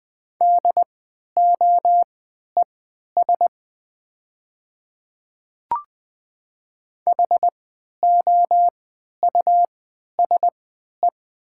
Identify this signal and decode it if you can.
Morse code, heard as a keyed tone, about 20 wpm, spaced as for 11 wpm, 700 Hz. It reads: DOES HOUSE
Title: 3x Character Spacing - Top 500 Words - 20wpm